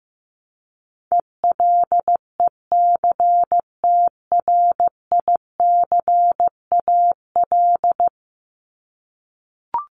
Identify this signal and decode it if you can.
Morse code, heard as a keyed tone, about 15 wpm, 700 Hz. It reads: ELECTRICAL
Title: Top 751-1000 Words 15wpm